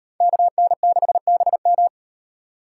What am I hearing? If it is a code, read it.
Morse code, heard as a keyed tone, about 36 wpm, 700 Hz. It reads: KN6BM